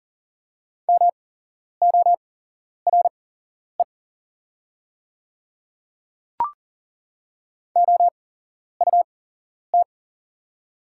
Morse code, heard as a keyed tone, about 40 wpm, spaced as for 11 wpm, 690 Hz. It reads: MORE OUT
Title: 8x Character Spacing - Top 100 Words - 40wpm